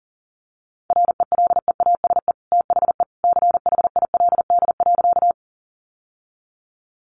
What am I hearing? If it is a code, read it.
Morse code, heard as a keyed tone, about 37 wpm, 700 Hz. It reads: RELEASE THE CHILD.
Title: Sentences from Top 1100 Words 40wpm